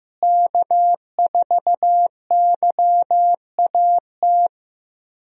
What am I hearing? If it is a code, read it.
Morse code, heard as a keyed tone, about 15 wpm, 700 Hz. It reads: K4YAT
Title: USA Call Signs - 5 Characters - Repeated 2x - 15wpm